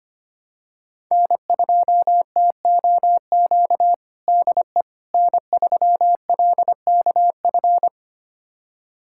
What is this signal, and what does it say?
Morse code, heard as a keyed tone, about 25 wpm, 700 Hz. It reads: N2TOQ DE N3LKF
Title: USA Callsign DE Callsign - 5 Characters - 25wpm